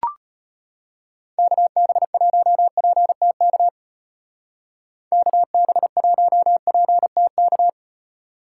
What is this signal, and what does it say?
Morse code, heard as a keyed tone, about 30 wpm, 700 Hz. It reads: KB1PTK KB1PTK
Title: Speed Racing - USA 2x3 Call Signs 25wpm